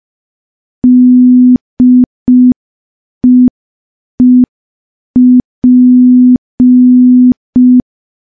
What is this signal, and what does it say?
Morse code, heard as a keyed tone, about 5 wpm, 250 Hz. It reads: DEEP
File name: morse / morse-training-05WPM